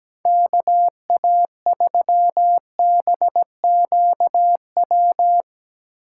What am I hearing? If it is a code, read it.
Morse code, heard as a keyed tone, about 17 wpm, 700 Hz. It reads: KA3BQW
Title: USA Call Signs - 6 Characters - 8x Character Spacing - 17wpm